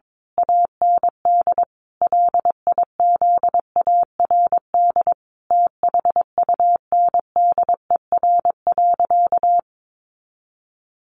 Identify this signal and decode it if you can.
Morse code, heard as a keyed tone, about 22 wpm, 700 Hz. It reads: AND LIZARD THUNDER.
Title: A Sound Of Thunder 22wpm